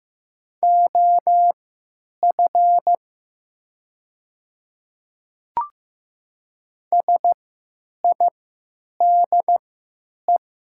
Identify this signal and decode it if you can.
Morse code, heard as a keyed tone, about 15 wpm, spaced as for 9 wpm, 700 Hz. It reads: OF SIDE
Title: 3x Character Spacing - Top 100 Words - 15wpm